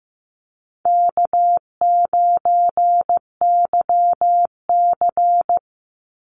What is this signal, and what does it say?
Morse code, heard as a keyed tone, about 15 wpm, 700 Hz. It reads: K9YC
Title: Easy Call Signs Repeated 3x - 15wpm